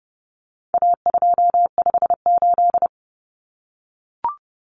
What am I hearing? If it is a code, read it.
Morse code, heard as a keyed tone, about 30 wpm, 700 Hz. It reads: A258
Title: Common Cut Numbers - 4-Digit 30wpm